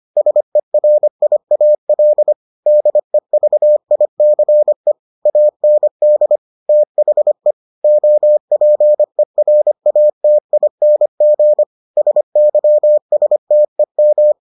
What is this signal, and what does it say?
Morse code, heard as a keyed tone, about 25 wpm, 600 Hz. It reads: SERIAL DEVICE AND THE OPERATING SYSTEM